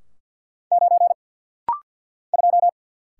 Morse code, heard as a keyed tone, about 48 wpm, 690 Hz. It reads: 9 2
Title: Single Letter-Number - Rapid-Fire 50wpm